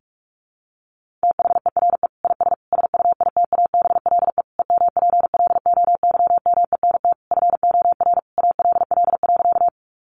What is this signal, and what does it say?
Morse code, heard as a keyed tone, about 44 wpm, 700 Hz. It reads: THERE IS SUITABLE EMPLOYMENT FOR ALL.
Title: Sentences from Top 1500 Words 45wpm